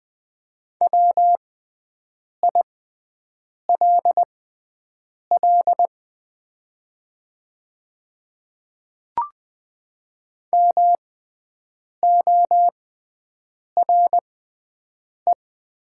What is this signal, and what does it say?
Morse code, heard as a keyed tone, about 20 wpm, spaced as for 7 wpm, 700 Hz. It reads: WILL MORE
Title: 6x Character Spacing - Top 100 Words - 20wpm